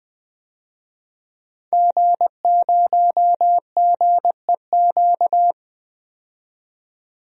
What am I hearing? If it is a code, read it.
Morse code, heard as a keyed tone, about 20 wpm, 700 Hz. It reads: G0GEQ
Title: DXCC-100 Call Signs - 5 Characters - 20wpm